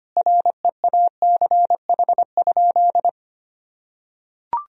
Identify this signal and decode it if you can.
Morse code, heard as a keyed tone, about 25 wpm, 700 Hz. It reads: REACH?
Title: Sentences from Top 1000 Words - Review Part 4 - 25wpm